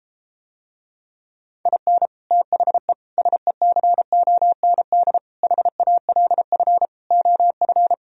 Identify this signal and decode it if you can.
Morse code, heard as a keyed tone, about 33 wpm, 700 Hz. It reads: IN THE SECOND HALF OF